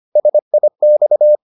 Morse code, heard as a keyed tone, about 25 wpm, 600 Hz. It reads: SIX